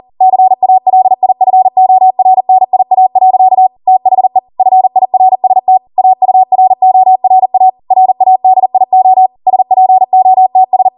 Morse code, heard as a keyed tone, about 40 wpm, 750 Hz. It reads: CALIFORNIA. THE FIRST AURORA RADIO SPOTS